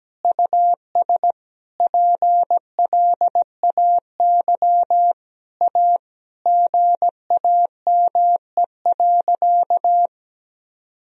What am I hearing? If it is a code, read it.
Morse code, heard as a keyed tone, about 17 wpm, 700 Hz. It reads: US PLAY A GAME.
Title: Sentences from Top 1000 Words - Review Part 4 - 17wpm